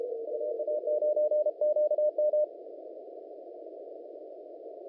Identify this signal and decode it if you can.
Morse code, heard as a keyed tone, about 34 wpm, 600 Hz. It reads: UA9QM
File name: chinese / FT991report / 17-44-43 Channel 1 clip